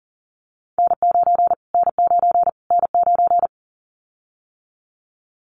Morse code, heard as a keyed tone, about 40 wpm, 700 Hz. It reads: N9 N9 N9
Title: Callsign Prefixes for Top 25 DXCC - Repeated 3x - 40wpm